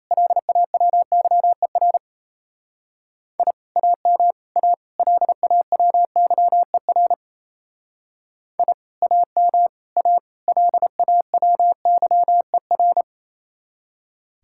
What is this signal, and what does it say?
Morse code, heard as a keyed tone, about 36 wpm, 700 Hz. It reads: LAWYER I AM A LAWYER I AM A LAWYER